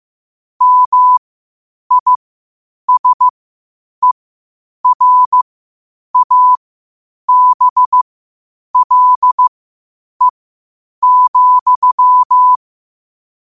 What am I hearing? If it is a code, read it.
Morse code, heard as a keyed tone, about 15 wpm, spaced as for 8 wpm, 1000 Hz. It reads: MISERABLE,
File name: qotdfiles / 8wpm